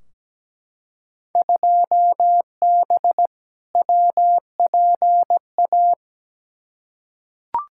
Morse code, heard as a keyed tone, about 17 wpm, 700 Hz. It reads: 2B WPA
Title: Field Day Exchanges Encore - 17wpm